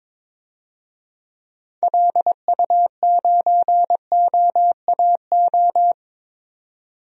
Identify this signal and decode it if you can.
Morse code, heard as a keyed tone, about 22 wpm, 700 Hz. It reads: LU9OAO
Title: DXCC-100 Call Signs - 6 Characters - 22wpm